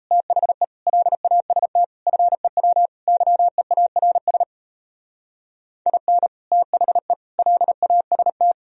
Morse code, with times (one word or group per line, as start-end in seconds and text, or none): 0.1-0.6 THE
0.9-1.8 LAST
2.1-2.9 FEW
3.1-4.4 YEARS
5.9-6.3 IN
6.5-7.1 THE
7.4-8.5 LAST